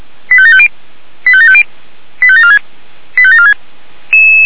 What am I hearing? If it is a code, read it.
Five-tone selective call (ZVEI), heard as quick runs of five tones, tones of about 70 ms. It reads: 85750 85750 85746 85746